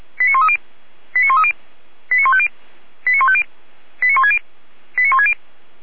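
Five-tone selective call (ZVEI), heard as quick runs of five tones, tones of about 80 ms. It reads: 89130 89130 89150 89150 89160 89160
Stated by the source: ZVEI five-tone call received by SDR.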